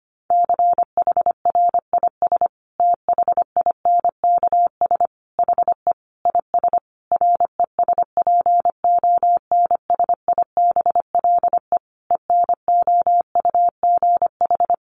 Morse code, heard as a keyed tone, about 25 wpm, 700 Hz. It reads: CHRIS THINKS HE IS RESPONSIBLE ENOUGH
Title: Additional Easy Phrases 25wpm